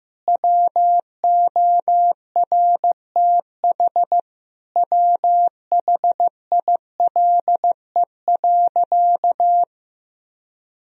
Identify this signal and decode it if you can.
Morse code, heard as a keyed tone, about 15 wpm, 700 Hz. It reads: WORTH WHILE.
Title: Sentences from Top 1000 Words - Review Part 1 - 15wpm